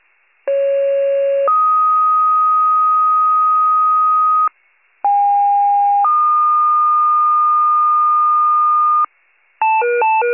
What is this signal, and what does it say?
Two-tone page, heard as a one-second tone, then a three-second tone, about 550 then 1250 Hz, about 810 then 1220 Hz. It is heard from 0.5 to 9.0 s.
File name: 28 FIRE